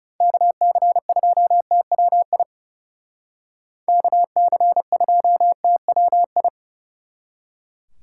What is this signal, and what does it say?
Morse code, heard as a keyed tone, about 38 wpm, 700 Hz. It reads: KC2TWI KC2TWI
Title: Speed Racing - USA Call Signs 30wpm